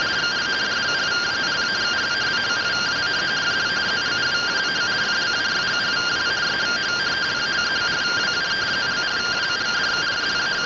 Two-tone teleprinter signal, about 170 Hz shift.